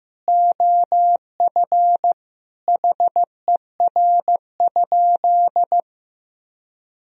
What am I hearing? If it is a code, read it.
Morse code, heard as a keyed tone, about 16 wpm, 700 Hz. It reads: OF HER?
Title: Sentences from Top 1000 Words - Review Part 2 - 15wpm